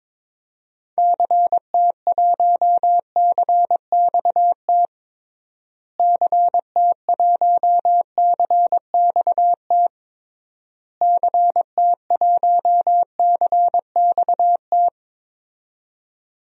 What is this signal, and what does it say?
Morse code, heard as a keyed tone, about 21 wpm, 700 Hz. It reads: CT1CXT CT1CXT CT1CXT
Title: DXCC-100 Call Signs - 6 Characters - Repeated 3x - 22wpm